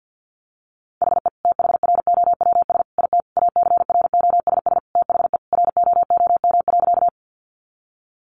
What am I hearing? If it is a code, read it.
Morse code, heard as a keyed tone, about 50 wpm, 700 Hz. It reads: HE THROWS IT ACROSS THE ROOM.
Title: Additional Easy Phrases 50wpm